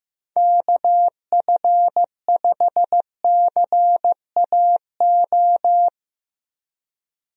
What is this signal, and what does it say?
Morse code, heard as a keyed tone, about 15 wpm, 700 Hz. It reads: KF5CAO
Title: USA Call Signs - 6 Characters - Repeated 3x - 15wpm